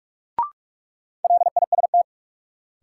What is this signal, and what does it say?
Morse code, heard as a keyed tone, about 46 wpm, 700 Hz. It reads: LIST
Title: Sentences from Top 400 Words - New Word Drill - 45wpm